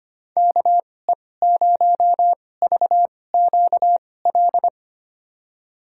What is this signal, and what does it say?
Morse code, heard as a keyed tone, about 25 wpm, spaced as for 18 wpm, 700 Hz. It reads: KE0VQL
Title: USA Call Signs - 6 Characters - 2x Character Spacing - 25wpm